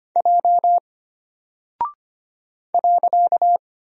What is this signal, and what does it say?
Morse code, heard as a keyed tone, about 25 wpm, 700 Hz. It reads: J .
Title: Single Character (Letter:Number:Punctuation:Prosign) Alt 25wpm